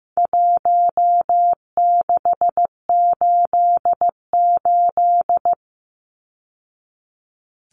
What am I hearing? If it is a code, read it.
Morse code, heard as a keyed tone, about 15 wpm, 700 Hz. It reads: 1688
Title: Numbers - Contest Style - 4-Digit 15wpm